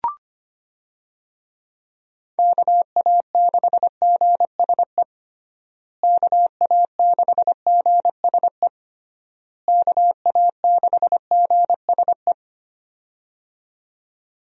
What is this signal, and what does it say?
Morse code, heard as a keyed tone, about 25 wpm, 700 Hz. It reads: KA6GSE KA6GSE KA6GSE
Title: USA Call Signs - 6 Characters - Repeated 3x - 25wpm